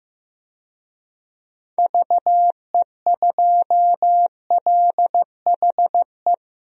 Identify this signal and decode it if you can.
Morse code, heard as a keyed tone, about 15 wpm, 700 Hz. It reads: VE2LHE
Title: DXCC-100 Call Signs - 6 Characters - 15wpm